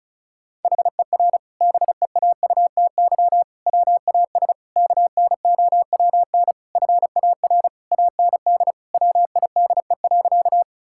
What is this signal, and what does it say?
Morse code, heard as a keyed tone, about 36 wpm, 700 Hz. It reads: HER BEAUTY WAS KNOWN FAR AND WIDE.
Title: Sentences from Top 1000 Words - Review Part 6 - 35wpm